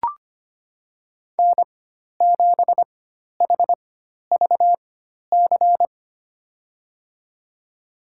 Morse code, heard as a keyed tone, about 25 wpm, spaced as for 12 wpm, 700 Hz. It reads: N7HVC
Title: USA Call Signs - 4x Character Spacing - 25wpm